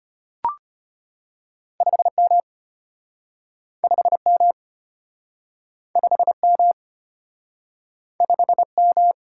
Morse code, heard as a keyed tone, about 34 wpm, 700 Hz. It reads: 5M 5M 5M 5M